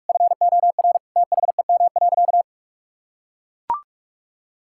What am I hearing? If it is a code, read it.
Morse code, heard as a keyed tone, about 44 wpm, 700 Hz. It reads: FOR THEM.
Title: Sentences from Top 1000 Words - Review Part 5 - 45wpm